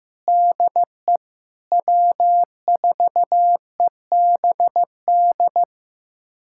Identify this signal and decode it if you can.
Morse code, heard as a keyed tone, about 16 wpm, 700 Hz. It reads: DE W4EBD
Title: USA Callsign DE Callsign 15wpm